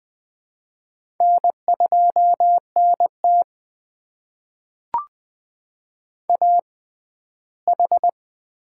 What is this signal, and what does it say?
Morse code, heard as a keyed tone, about 20 wpm, spaced as for 14 wpm, 700 Hz. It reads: N2NT AH
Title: USA Call Signs - 4 Characters - 6x Character Spacing - 20wpm